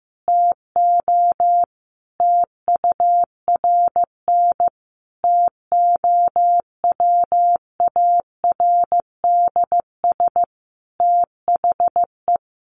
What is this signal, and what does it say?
Morse code, heard as a keyed tone, about 15 wpm, 700 Hz. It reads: TO TURN TOWARDS THE
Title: Sentences from Top 1100 Words 15wpm